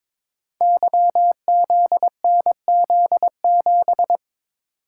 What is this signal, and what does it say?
Morse code, heard as a keyed tone, about 22 wpm, 700 Hz. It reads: YZNZ7
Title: Code Groups - 5 Letter:Numbers - 22wpm